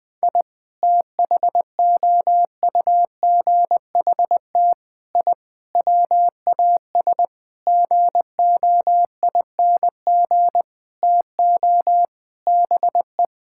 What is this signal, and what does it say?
Morse code, heard as a keyed tone, about 20 wpm, 700 Hz. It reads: I THOUGHT I WAS GOING TO BE
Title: Sentences with 6-Word Phrases - Spoken Beginning - 20wpm